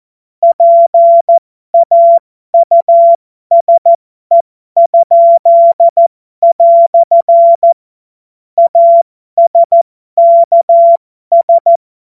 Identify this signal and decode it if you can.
Morse code, heard as a keyed tone, about 14 wpm, spaced as for 12 wpm, 680 Hz. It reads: PAUSE?" ASKS
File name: Morsekurs / Witze